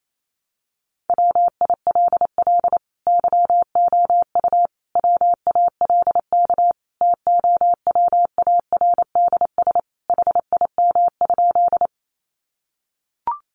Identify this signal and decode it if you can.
Morse code, heard as a keyed tone, about 28 wpm, 700 Hz. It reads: WILL YOU WALK TOWARDS HIM?
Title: Sentences from Top 1100 Words 28wpm